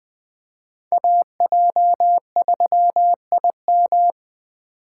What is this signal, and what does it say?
Morse code, heard as a keyed tone, about 20 wpm, 700 Hz. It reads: AJ3IM
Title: USA Call Signs - 5 Characters - 2x Character Spacing - 20wpm